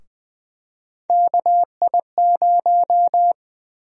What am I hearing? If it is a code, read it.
Morse code, heard as a keyed tone, about 21 wpm, 700 Hz. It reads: KI0
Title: USA Call Sign Prefixes - Repeated 2x - 20wpm